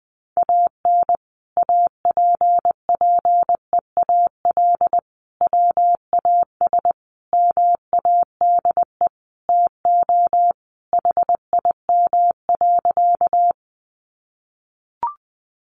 Morse code, heard as a keyed tone, about 20 wpm, 700 Hz. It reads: AN APPEAL WAS MADE TO HIM.
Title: Sentences from Top 1300 Words 20wpm